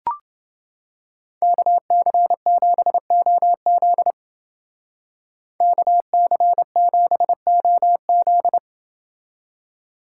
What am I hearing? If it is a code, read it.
Morse code, heard as a keyed tone, about 28 wpm, 700 Hz. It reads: KC7OZ KC7OZ